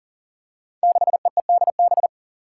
Morse code, heard as a keyed tone, about 40 wpm, 700 Hz. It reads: 6EEDB
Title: Five Challenging Letter-Number Combinations 40wpm